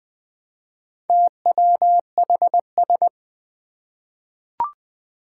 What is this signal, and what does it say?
Morse code, heard as a keyed tone, about 20 wpm, 700 Hz. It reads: TWHS
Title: Four Challenging Letter-Number Combinations 20wpm